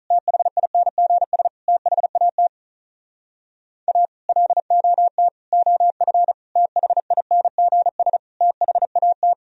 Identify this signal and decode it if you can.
Morse code, heard as a keyed tone, about 46 wpm, 700 Hz. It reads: THINGS THAT A LOT OF THINGS THAT